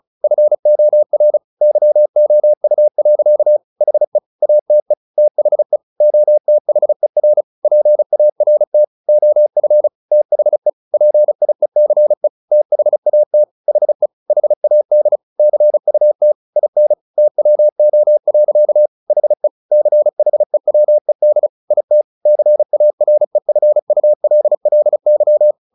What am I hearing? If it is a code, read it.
Morse code, heard as a keyed tone, about 34 wpm, 600 Hz. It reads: FOR YOU. HE ATE THE OTHER PART OF THE PIECE THAT HE HAD CUT IN TWO. HE CHEWED IT CAREFULLY
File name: the old man and the sea / the old man and the sea 1 35wpm